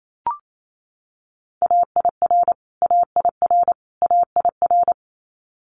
Morse code, heard as a keyed tone, about 28 wpm, 700 Hz. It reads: AIR AIR AIR
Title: Top 200 Words Repeated 28wpm